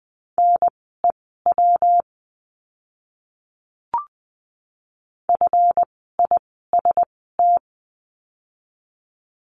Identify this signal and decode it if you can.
Morse code, heard as a keyed tone, about 20 wpm, spaced as for 15 wpm, 700 Hz. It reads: NEW FIST
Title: 139 Lesson 23f - Letter f - Words 20wpm